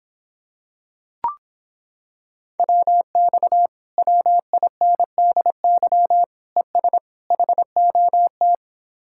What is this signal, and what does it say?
Morse code, heard as a keyed tone, about 26 wpm, 700 Hz. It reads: WX WINDY ES HOT